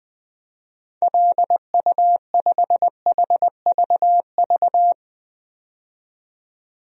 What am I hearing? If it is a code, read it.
Morse code, heard as a keyed tone, about 20 wpm, 700 Hz. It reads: LU5HVV
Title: DXCC-100 Call Signs - 6 Characters - 20wpm